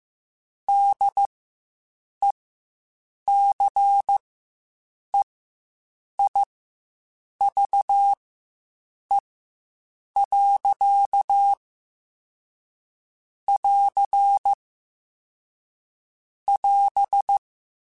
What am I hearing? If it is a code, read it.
Morse code, heard as a keyed tone, about 15 wpm, spaced as for 7 wpm, 800 Hz. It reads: DECEIVE. <AR> <AS>
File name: Quotes Spaced / Quotes Spaced-mp3s / Deceive sp @10